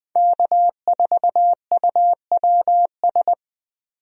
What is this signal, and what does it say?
Morse code, heard as a keyed tone, about 20 wpm, 700 Hz. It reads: K4UWS
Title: USA Call Signs - 5 Characters - 20wpm